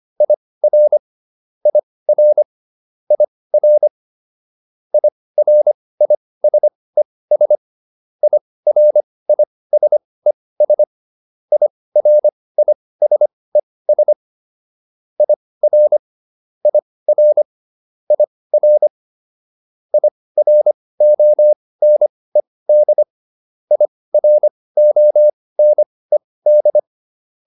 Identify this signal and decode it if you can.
Morse code, heard as a keyed tone, about 25 wpm, spaced as for 18 wpm, 600 Hz. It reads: IR IR IR IRISES IRISES IRISES IR IR IR IRONED IRONED